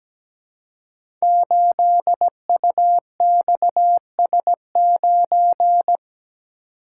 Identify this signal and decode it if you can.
Morse code, heard as a keyed tone, about 17 wpm, 700 Hz. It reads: 8UXS9
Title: Code Groups - 5 Letter:Numbers - 17wpm